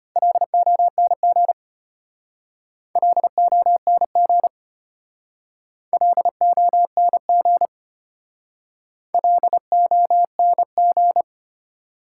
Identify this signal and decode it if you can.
Morse code, heard as a keyed tone, about 34 wpm, 700 Hz. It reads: LONG LONG LONG LONG